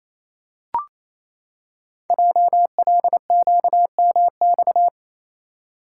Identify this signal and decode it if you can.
Morse code, heard as a keyed tone, about 27 wpm, 700 Hz. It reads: JLQMX
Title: Code Groups - 5 Letters - 28wpm